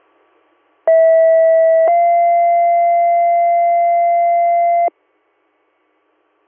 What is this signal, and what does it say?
Two-tone page, heard as a one-second tone, then a three-second tone, about 650 then 690 Hz. It is heard from 0.9 to 4.9 s.